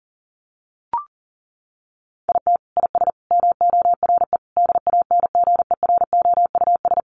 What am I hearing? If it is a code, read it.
Morse code, heard as a keyed tone, about 40 wpm, 700 Hz. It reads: IT IS MORE DANGEROUS
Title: Sentences from Top 1200 Words 40wpm